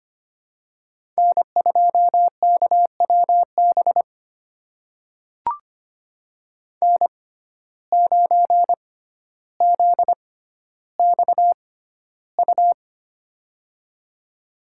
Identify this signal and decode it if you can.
Morse code, heard as a keyed tone, about 25 wpm, spaced as for 13 wpm, 700 Hz. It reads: N2KWB N9ZXU
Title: USA Call Signs - 5 Characters - 6x Character Spacing - 25wpm